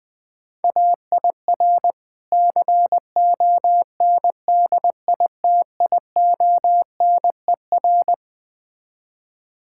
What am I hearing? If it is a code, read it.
Morse code, heard as a keyed tone, about 20 wpm, 700 Hz. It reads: AIR CONDITIONER
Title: Difficult Words - Speed-Racing with Hints - 20wpm